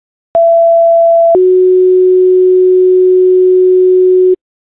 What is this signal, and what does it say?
Two-tone page, heard as a one-second tone, then a three-second tone, about 670 then 370 Hz, with noise. It is heard from 0.3 to 4.4 s.